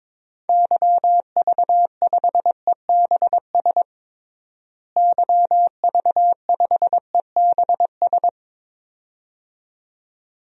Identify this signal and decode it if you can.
Morse code, heard as a keyed tone, about 21 wpm, 700 Hz. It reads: YV5EBS YV5EBS
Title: DXCC-100 Call Signs - 6 Characters - Repeated 2x - 22wpm